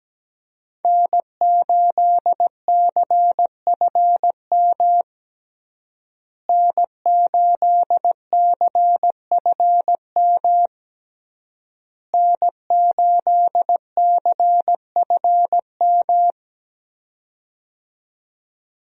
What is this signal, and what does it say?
Morse code, heard as a keyed tone, about 17 wpm, 700 Hz. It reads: N8CFM N8CFM N8CFM
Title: USA Call Signs - 5 Characters - Repeated 3x - 17wpm